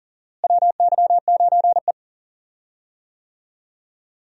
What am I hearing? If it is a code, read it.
Morse code, heard as a keyed tone, about 39 wpm, 700 Hz. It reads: WY9E
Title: USA Call Signs - 4 Characters - Repeated 2x - 40wpm